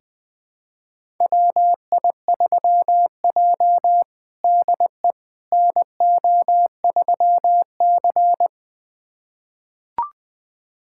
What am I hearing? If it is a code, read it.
Morse code, heard as a keyed tone, about 21 wpm, 700 Hz. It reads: WI3J DE NO3C
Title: USA Callsign DE Callsign - 4 Characters - 20wpm